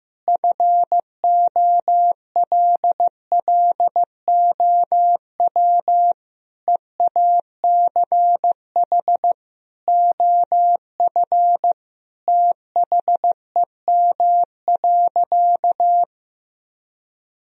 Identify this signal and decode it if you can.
Morse code, heard as a keyed tone, about 16 wpm, 700 Hz. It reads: FOLLOW EACH OF THEM.
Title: Sentences from Top 1000 Words - Review Part 1 - 15wpm